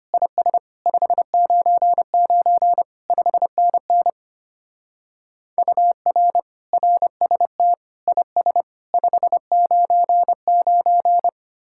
Morse code, heard as a keyed tone, about 31 wpm, 700 Hz. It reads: IS 599 5NN UR RST IS 599
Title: Speed-Racing Common QSO Phrases - 25wpm